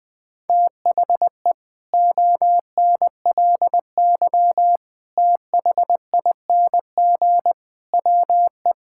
Morse code, heard as a keyed tone, about 21 wpm, 700 Hz. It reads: THE ONLY THING WE
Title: Sets of 8 Words - 20wpm